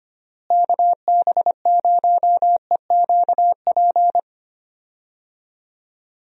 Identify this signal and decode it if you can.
Morse code, heard as a keyed tone, about 25 wpm, 700 Hz. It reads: KB0EQP
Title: USA Call Signs - 6 Characters - Repeated 2x - 25wpm